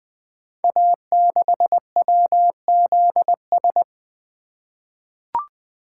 Morse code, heard as a keyed tone, about 20 wpm, 700 Hz. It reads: A6WZS
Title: Code Groups - 5 Letter:Numbers - 20wpm